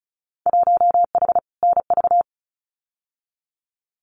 Morse code, heard as a keyed tone, about 34 wpm, 700 Hz. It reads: 1H NV
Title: Winter Field Day Exchanges 35wpm